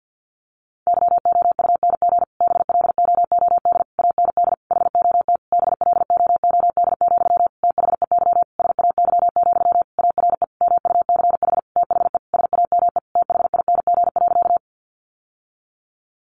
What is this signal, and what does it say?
Morse code, heard as a keyed tone, about 49 wpm, 700 Hz. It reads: YOUNG BLOOD AND HOT BLOOD, THEY SAY, ARE MUCH THE SAME THING.
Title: Sentences from Top 1100 Words 50wpm